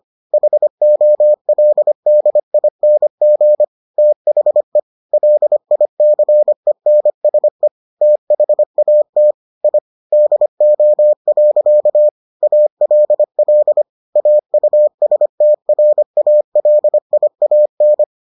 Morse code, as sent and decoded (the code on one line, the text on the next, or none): .... --- .-.. -.. .. -. --. / - .... / .-.. .. -.-. -. ... / - .... .- - / .. / -.. --- .-.-.- / .- .-.. .-.. / .- ..- ... - .-. .- .-.. .. .- -.
HOLDING THE LICENSE THAT I DO. ALL AUSTRALIAN